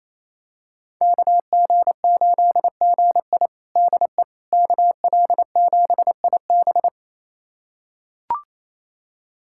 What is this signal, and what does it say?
Morse code, heard as a keyed tone, about 28 wpm, 700 Hz. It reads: KG8GI DE KL7IB
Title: USA Callsign DE Callsign 28wpm